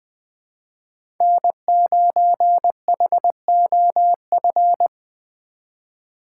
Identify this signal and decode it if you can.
Morse code, heard as a keyed tone, about 21 wpm, 700 Hz. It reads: N9HOF
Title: USA Call Signs - 5 Characters - 4x Character Spacing - 20wpm